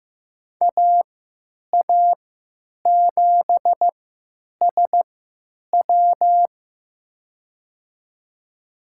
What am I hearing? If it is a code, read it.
Morse code, heard as a keyed tone, about 15 wpm, spaced as for 9 wpm, 700 Hz. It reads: AA7SW
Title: USA Call Signs - 5 Characters - 3x Character Spacing - 15wpm